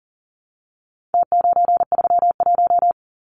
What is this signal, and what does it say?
Morse code, heard as a keyed tone, about 40 wpm, 700 Hz. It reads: T931
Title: Numbers - Contest Style - 4-Digit 40wpm